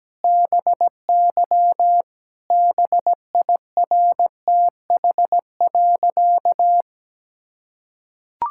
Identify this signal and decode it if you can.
Morse code, heard as a keyed tone, about 17 wpm, 700 Hz. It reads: BY BIRTH.
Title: Sentences from Top 1000 Words - Review Part 5 - 17wpm